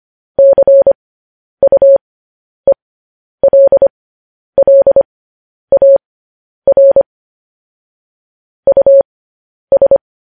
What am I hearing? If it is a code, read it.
Morse code, heard as a keyed tone, about 25 wpm, spaced as for 10 wpm, 550 Hz. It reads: CUELLAR US
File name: podcast / News headlines-1x-25x10